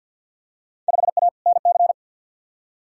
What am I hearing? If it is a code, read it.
Morse code, heard as a keyed tone, about 53 wpm, 700 Hz. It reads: 5A NC
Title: Field Day Exchanges - 50wpm